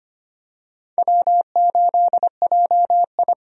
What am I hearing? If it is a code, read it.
Morse code, heard as a keyed tone, about 25 wpm, 700 Hz. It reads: W8JI
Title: USA Call Signs - 4 Characters - 25wpm